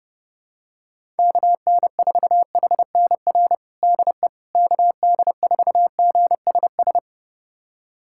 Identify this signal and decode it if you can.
Morse code, heard as a keyed tone, about 32 wpm, 700 Hz. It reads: KN4HNR DE KD4GSS
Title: USA Callsign DE Callsign - 6 Characters - 30wpm